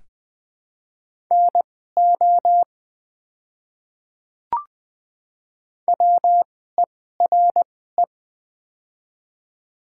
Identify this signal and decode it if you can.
Morse code, heard as a keyed tone, about 20 wpm, spaced as for 15 wpm, 700 Hz. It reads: NO WERE
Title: 2x Character Spacing - Top 200 Words - 20wpm